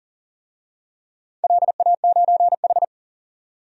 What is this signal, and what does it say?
Morse code, heard as a keyed tone, about 40 wpm, 700 Hz. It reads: LA9H
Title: DXCC-100 Call Signs - 4 Characters - 40wpm